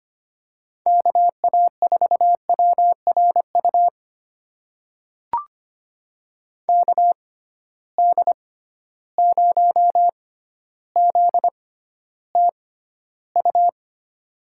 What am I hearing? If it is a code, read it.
Morse code, heard as a keyed tone, about 25 wpm, spaced as for 13 wpm, 700 Hz. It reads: KA4WRU KD0ZTU
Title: USA Call Signs - 6 Characters - 6x Character Spacing - 25wpm